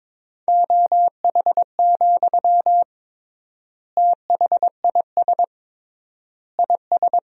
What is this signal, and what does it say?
Morse code, heard as a keyed tone, about 21 wpm, 700 Hz. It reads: OH, THIS IS
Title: 3x Word-Spacing - Sentences from Top 500 Words - 22wpm